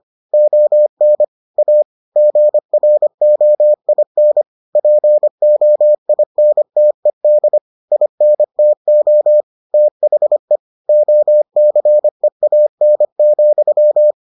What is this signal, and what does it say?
Morse code, heard as a keyed tone, about 25 wpm, 600 Hz. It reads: ON A GROIN POINTED INTO THE OCEAN,